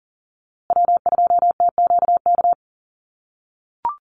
Morse code, heard as a keyed tone, about 40 wpm, 700 Hz. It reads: W2TQK
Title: USA Call Signs Repeated 3x 40wpm